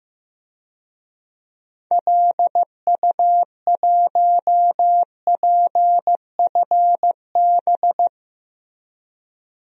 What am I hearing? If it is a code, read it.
Morse code, heard as a keyed tone, about 15 wpm, 700 Hz. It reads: LU1PFB
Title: DXCC-100 Call Signs - 6 Characters - Repeated 2x - 15wpm